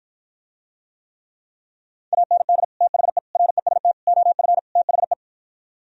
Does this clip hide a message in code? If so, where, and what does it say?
Yes: Morse code from 2.1 to 5.1 s, heard as a keyed tone, about 52 wpm, 700 Hz. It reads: AND THE REST OF THE